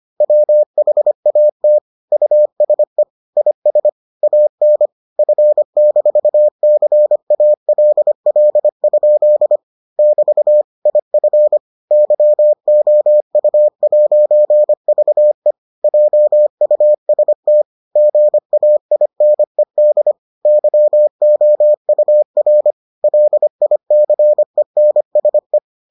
Morse code, heard as a keyed tone, about 25 wpm, 600 Hz. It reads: WHAT USE IS AN F-CALL? <BT> IF YOU'VE JUST GAINED YOUR LICENSE